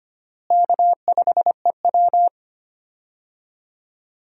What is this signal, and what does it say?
Morse code, heard as a keyed tone, about 25 wpm, 700 Hz. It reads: K5EW
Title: USA Call Signs - 4 Characters - Repeated 3x - 25wpm